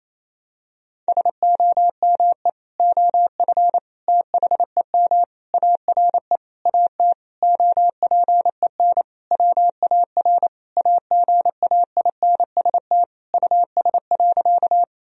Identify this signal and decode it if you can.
Morse code, heard as a keyed tone, about 28 wpm, 700 Hz. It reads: SOME OF THEM ARE AT OPEN WAR AGAINST US.
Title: Sentences from Top 1000 Words - Review Part 2 - 28wpm